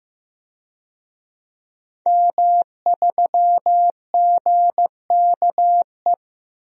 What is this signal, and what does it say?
Morse code, heard as a keyed tone, about 16 wpm, 700 Hz. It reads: M3GKE
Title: DXCC-100 Call Signs - 5 Characters - 15wpm